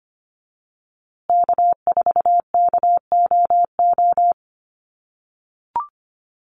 Morse code, heard as a keyed tone, about 25 wpm, 700 Hz. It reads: K4KOO
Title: USA Call Signs Repeated 2x 25wpm